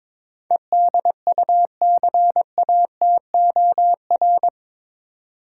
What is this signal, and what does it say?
Morse code, heard as a keyed tone, about 22 wpm, 700 Hz. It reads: EDUCATOR
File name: practice / 22wpm / Eight Letter Words - 22wpm